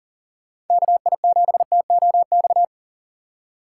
Morse code, heard as a keyed tone, about 48 wpm, 700 Hz. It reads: KI7TOX
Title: Speed Racing - USA 2x3 Call Signs 40wpm